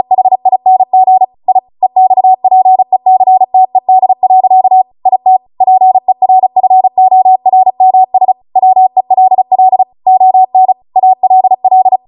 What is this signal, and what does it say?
Morse code, heard as a keyed tone, about 36 wpm, 750 Hz. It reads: HING I EXPECTED. IT PERFORMS WELL ON ALL